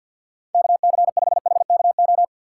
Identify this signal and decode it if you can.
Morse code, heard as a keyed tone, about 52 wpm, 700 Hz. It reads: KC5HKO
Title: USA Call Signs - 6 Characters - 6x Character Spacing - 50wpm